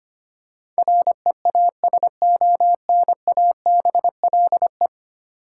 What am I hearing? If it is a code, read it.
Morse code, heard as a keyed tone, about 25 wpm, 700 Hz. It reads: REASONABLE
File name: 10 Letter Words - First 3 Letters Given - 25wpm